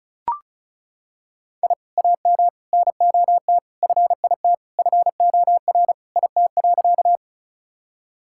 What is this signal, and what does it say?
Morse code, heard as a keyed tone, about 36 wpm, 700 Hz. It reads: I AM NOT FIT FOR IT.
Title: Sentences from Top 1000 Words - Review Part 5 - 35wpm